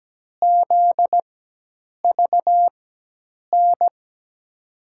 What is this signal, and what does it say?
Morse code, heard as a keyed tone, about 17 wpm, spaced as for 8 wpm, 700 Hz. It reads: ZVN
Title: USA Call Signs - 6 Characters - 4x Character Spacing - 17wpm